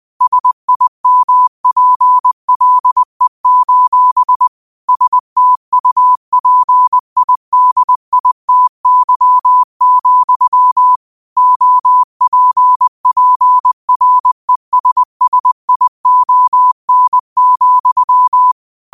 Morse code, heard as a keyed tone, about 20 wpm, 1000 Hz. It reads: SIMPLE: STUPIDITY, OPPRESSION,
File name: qotdfiles / 20wpm